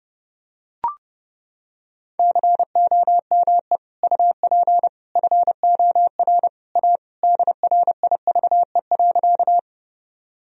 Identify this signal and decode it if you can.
Morse code, heard as a keyed tone, about 32 wpm, 700 Hz. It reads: COME UP FOR A DRIVE.
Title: Sentences from Top 1000 Words - Review Part 2 - 30wpm